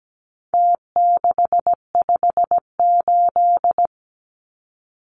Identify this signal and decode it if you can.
Morse code, heard as a keyed tone, about 17 wpm, 700 Hz. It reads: T658
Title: Numbers - Contest Style - 4-Digit 17wpm